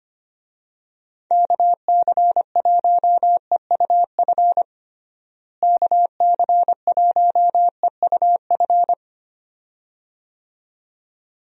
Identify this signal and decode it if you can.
Morse code, heard as a keyed tone, about 25 wpm, 700 Hz. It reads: KC1EUF KC1EUF
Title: USA Call Signs - 6 Characters - Repeated 2x - 25wpm